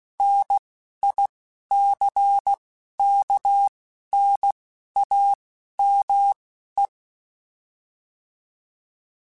Morse code, heard as a keyed tone, about 16 wpm, spaced as for 11 wpm, 800 Hz. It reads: NICKNAME
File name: Technical Subjects / Technical Subjects-mp3s / AND Gate @10